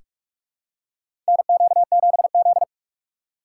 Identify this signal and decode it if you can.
Morse code, heard as a keyed tone, about 43 wpm, 700 Hz. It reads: NQ7Z
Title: USA Call Signs - 6x Character Spacing - 45wpm